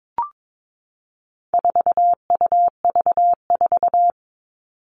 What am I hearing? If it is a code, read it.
Morse code, heard as a keyed tone, about 22 wpm, 700 Hz. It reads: 4UV4
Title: All Cut Numbers - 4-Digit 22wpm